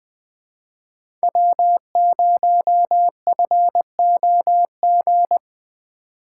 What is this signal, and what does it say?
Morse code, heard as a keyed tone, about 20 wpm, 700 Hz. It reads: W0FOG
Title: USA Call Signs - 5 Characters - 2x Character Spacing - 20wpm